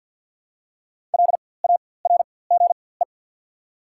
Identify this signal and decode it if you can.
Morse code, heard as a keyed tone, about 53 wpm, spaced as for 24 wpm, 690 Hz. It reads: LARGE
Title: 4x Character Spacing - Top 300 Words - 50wpm